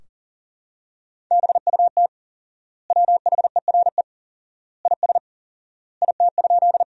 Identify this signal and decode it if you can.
Morse code, heard as a keyed tone, about 40 wpm, 700 Hz. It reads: BUT WHERE IS IT?
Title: 4x Word-Spacing - Sentences from Top 200 Words - 40wpm